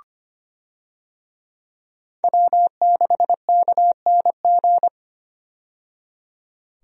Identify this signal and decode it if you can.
Morse code, heard as a keyed tone, about 25 wpm, 700 Hz. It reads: W6KNG
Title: USA Call Signs - 5 Characters - 25wpm